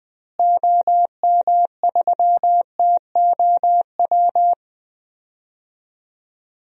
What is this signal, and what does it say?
Morse code, heard as a keyed tone, about 20 wpm, 700 Hz. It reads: OM3TOW
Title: DXCC-100 Call Signs - 6 Characters - Repeated 2x - 20wpm